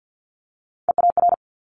Morse code, heard as a keyed tone, about 50 wpm, 700 Hz. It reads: EAR
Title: Multi-letter Phonemes Quick Fire 50wpm